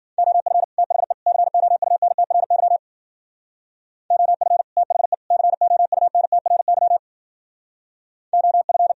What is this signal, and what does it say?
Morse code, heard as a keyed tone, about 51 wpm, 700 Hz. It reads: OF THE COUNTRY OF THE COUNTRY OF